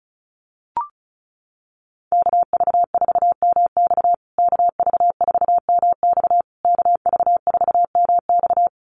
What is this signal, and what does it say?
Morse code, heard as a keyed tone, about 34 wpm, 700 Hz. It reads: KV4MX KV4MX KV4MX
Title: USA Call Signs Repeated 3x 35wpm